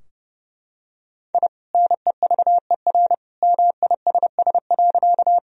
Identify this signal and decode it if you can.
Morse code, heard as a keyed tone, about 30 wpm, 700 Hz. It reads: I NEVER MISS.
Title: Sentences from Top 1000 Words - Review Part 3 - 30wpm